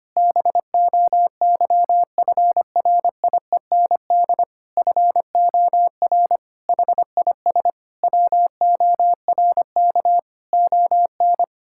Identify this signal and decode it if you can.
Morse code, heard as a keyed tone, about 25 wpm, 700 Hz. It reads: BOYFRIEND FOR HIS WORK ON
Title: Sentences from Top 1000 Words - Review Part 5 - 25wpm